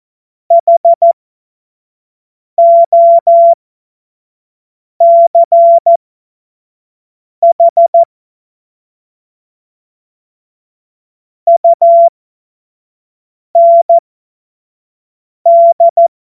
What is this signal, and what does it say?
Morse code, heard as a keyed tone, about 14 wpm, spaced as for 5 wpm, 680 Hz. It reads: HOCH UND
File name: Morsekurs / Witze / Witz 12. Oktober 2021, 25 F70